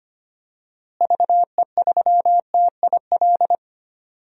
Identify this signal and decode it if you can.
Morse code, heard as a keyed tone, about 25 wpm, 700 Hz. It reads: VE3TIL
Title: DXCC-100 Call Signs - 6 Characters - Repeated 3x - 25wpm